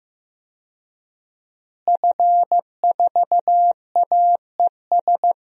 Morse code, heard as a keyed tone, about 15 wpm, 700 Hz. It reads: F4AES
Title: DXCC-100 Call Signs - 5 Characters - Repeated 2x - 15wpm